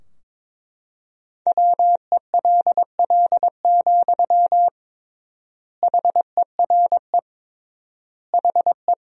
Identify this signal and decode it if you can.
Morse code, heard as a keyed tone, about 22 wpm, 700 Hz. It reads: WELL, HERE HE
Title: 3x Word-Spacing - Sentences from Top 200 Words - 22wpm